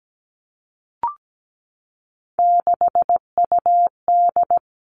Morse code, heard as a keyed tone, about 17 wpm, 700 Hz. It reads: 6UD
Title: Three Challenging Letter-Number Combinations 17wpm